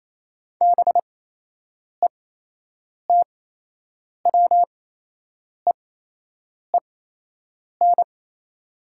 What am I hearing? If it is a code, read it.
Morse code, heard as a keyed tone, about 28 wpm, spaced as for 8 wpm, 700 Hz. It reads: BETWEEN
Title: 8x Character Spacing - Top 500 Words - 28wpm